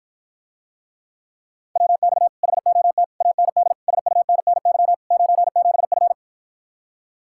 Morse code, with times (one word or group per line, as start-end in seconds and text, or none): 1.8-2.3 WX
2.4-3.0 HOT
3.2-3.7 AND
3.9-4.9 SUNNY
5.1-6.1 97F